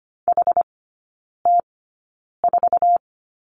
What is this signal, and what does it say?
Morse code, heard as a keyed tone, about 25 wpm, spaced as for 9 wpm, 700 Hz. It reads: HT4
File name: Lesson 17e - Letter u - Three Characters 25wpm